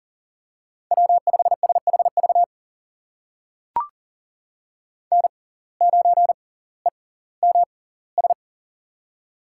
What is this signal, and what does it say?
Morse code, heard as a keyed tone, about 40 wpm, spaced as for 20 wpm, 700 Hz. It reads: W5SHV N9EMS
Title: USA Call Signs - 5 Characters - 6x Character Spacing - 40wpm